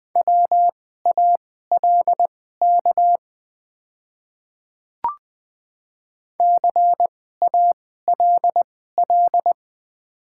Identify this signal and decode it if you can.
Morse code, heard as a keyed tone, about 20 wpm, spaced as for 15 wpm, 700 Hz. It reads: WALK CALL
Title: 2x Character Spacing - Top 300 Words - 20wpm